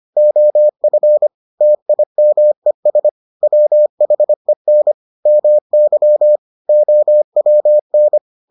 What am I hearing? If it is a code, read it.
Morse code, heard as a keyed tone, about 25 wpm, 600 Hz. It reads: OF TIMES WHEN MY OWN